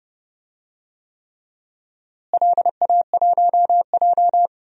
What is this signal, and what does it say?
Morse code, heard as a keyed tone, about 30 wpm, 700 Hz. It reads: LA1J
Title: DXCC-100 Call Signs - 4 Characters - Repeated 3x - 30wpm